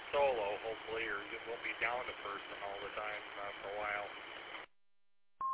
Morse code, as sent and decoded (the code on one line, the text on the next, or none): none